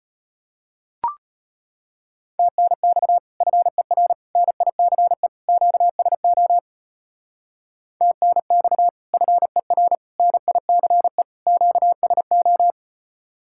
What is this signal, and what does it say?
Morse code, heard as a keyed tone, about 33 wpm, 700 Hz. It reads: TNX FER NICE QSO TNX FER NICE QSO